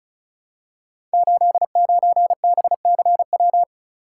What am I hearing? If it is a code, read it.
Morse code, heard as a keyed tone, about 36 wpm, 700 Hz. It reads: 89BCW
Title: Code Groups - 5 Letter:Numbers - 35wpm